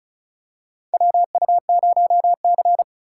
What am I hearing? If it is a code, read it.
Morse code, heard as a keyed tone, about 36 wpm, 700 Hz. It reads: WU0C
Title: USA Call Signs - 4 Characters - 6x Character Spacing - 35wpm